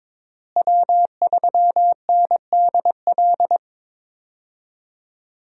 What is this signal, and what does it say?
Morse code, heard as a keyed tone, about 22 wpm, 700 Hz. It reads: W3NDL
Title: USA Call Signs - 5 Characters - Repeated 3x - 22wpm